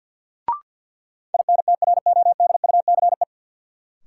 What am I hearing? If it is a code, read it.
Morse code, heard as a keyed tone, about 48 wpm, 700 Hz. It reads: INTRODUCE
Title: Sentences from Top 1600 Words - New Word Drill Alt - 50wpm